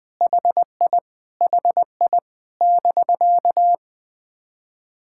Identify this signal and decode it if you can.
Morse code, heard as a keyed tone, about 20 wpm, 700 Hz. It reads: HI HI <BK>
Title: POTA QSO Practice - 20wpm